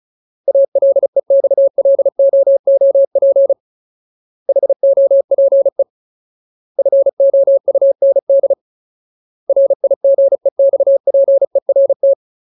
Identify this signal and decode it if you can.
Morse code, heard as a keyed tone, about 34 wpm, 550 Hz. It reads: ALEXLOOP HOPE FOUND RIGEXPERT
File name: Ultimate Word List / 35WPM Ultimate